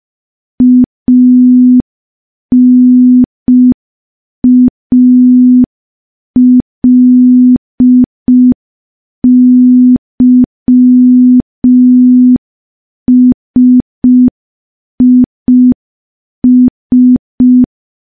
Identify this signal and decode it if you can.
Morse code, heard as a keyed tone, about 5 wpm, 250 Hz. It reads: ANALYSIS
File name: morse / morse-training-05WPM